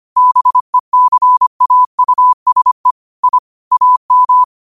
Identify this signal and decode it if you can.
Morse code, heard as a keyed tone, about 25 wpm, 1000 Hz. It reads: BECAUSE I AM